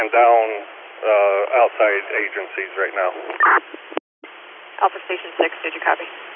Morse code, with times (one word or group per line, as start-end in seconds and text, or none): none